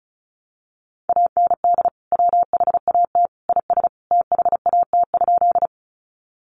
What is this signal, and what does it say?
Morse code, heard as a keyed tone, about 34 wpm, 700 Hz. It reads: AND WHAT IS THAT?
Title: Sentences from Top 100 Words 35wpm